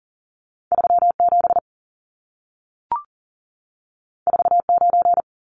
Morse code, heard as a keyed tone, about 40 wpm, 700 Hz. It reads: 37 49
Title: Numbers - 2-Digit 40wpm